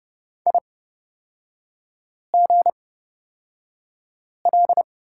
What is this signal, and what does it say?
Morse code, heard as a keyed tone, about 30 wpm, spaced as for 5 wpm, 700 Hz. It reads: IGL
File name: Lesson 38e - Letter q - Three Characters 30wpm